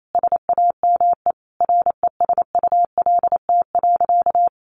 Morse code, heard as a keyed tone, about 28 wpm, 700 Hz. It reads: SAME RESULT.
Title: Sentences from Top 700 Words 28wpm